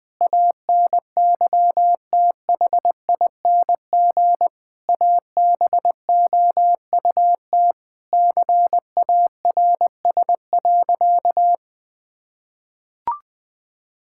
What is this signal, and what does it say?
Morse code, heard as a keyed tone, about 20 wpm, 700 Hz. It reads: ANYTHING ABOUT CARS.
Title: Sentences with 5-Word Phrases - 20wpm